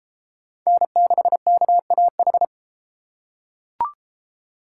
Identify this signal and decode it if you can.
Morse code, heard as a keyed tone, about 32 wpm, 700 Hz. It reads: N6KAH